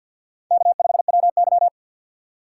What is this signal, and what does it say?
Morse code, heard as a keyed tone, about 64 wpm, 700 Hz. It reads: K5WY